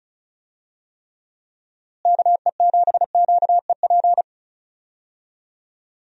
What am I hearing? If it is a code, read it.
Morse code, heard as a keyed tone, about 36 wpm, 700 Hz. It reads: KE7QEP
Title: USA Call Signs - 6 Characters - Repeated 3x - 35wpm